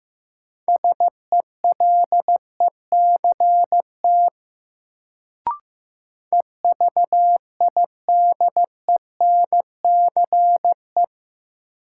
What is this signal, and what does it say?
Morse code, heard as a keyed tone, about 16 wpm, 700 Hz. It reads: SELECT EVIDENCE
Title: Sentences from Top 1300 Words - New Word Drill - 15wpm